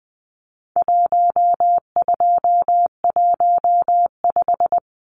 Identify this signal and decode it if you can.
Morse code, heard as a keyed tone, about 20 wpm, 700 Hz. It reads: 1215
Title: Numbers - 4-Digit 20wpm